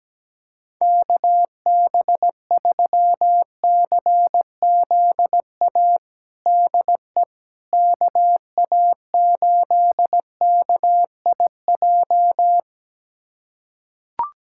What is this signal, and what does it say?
Morse code, heard as a keyed tone, about 17 wpm, 700 Hz. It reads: KB3CZA DE KA8KIJ
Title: USA Callsign DE Callsign - 6 Characters - 17wpm